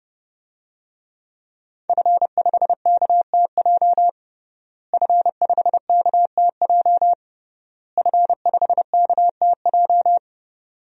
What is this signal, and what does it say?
Morse code, heard as a keyed tone, about 30 wpm, 700 Hz. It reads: F5KTJ F5KTJ F5KTJ
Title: DXCC-100 Call Signs - 5 Characters - Repeated 3x - 30wpm